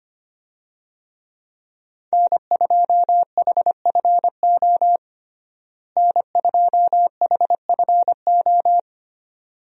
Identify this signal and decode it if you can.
Morse code, heard as a keyed tone, about 25 wpm, 700 Hz. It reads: N2HFO N2HFO
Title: USA Call Signs - 5 Characters - Repeated 2x - 25wpm